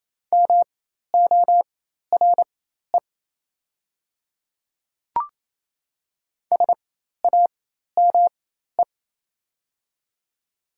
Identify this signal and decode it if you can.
Morse code, heard as a keyed tone, about 28 wpm, spaced as for 13 wpm, 700 Hz. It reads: MORE SAME
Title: 4x Character Spacing - Top 200 Words - 28wpm